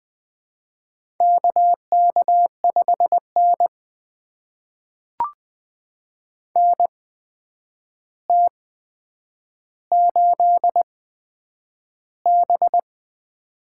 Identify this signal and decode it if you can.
Morse code, heard as a keyed tone, about 20 wpm, spaced as for 9 wpm, 700 Hz. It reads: KK5N NT8B
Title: USA Call Signs - 4 Characters - 8x Character Spacing - 20wpm